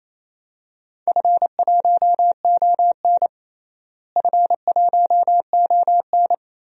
Morse code, heard as a keyed tone, about 28 wpm, 700 Hz. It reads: F1ON F1ON
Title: DXCC-100 Call Signs - 4 Characters - Repeated 3x - 28wpm